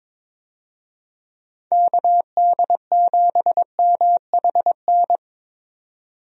Morse code, heard as a keyed tone, about 22 wpm, 700 Hz. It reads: KD7MHN
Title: USA Call Signs - 6 Characters - 22wpm